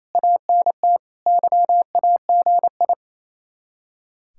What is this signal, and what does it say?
Morse code, heard as a keyed tone, about 36 wpm, 700 Hz. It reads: ANT YAGI